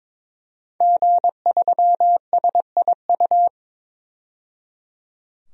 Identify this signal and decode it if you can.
Morse code, heard as a keyed tone, about 22 wpm, 700 Hz. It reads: G3SIU
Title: DXCC-100 Call Signs - 5 Characters - 22wpm